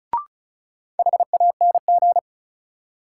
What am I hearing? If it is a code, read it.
Morse code, heard as a keyed tone, about 36 wpm, 700 Hz. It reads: HANG
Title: Sentences from Top 1600 Words - New Word Drill Alt - 35wpm